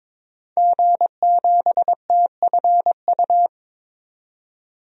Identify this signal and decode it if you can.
Morse code, heard as a keyed tone, about 22 wpm, 700 Hz. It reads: G7TFU
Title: DXCC-100 Call Signs - 5 Characters - 22wpm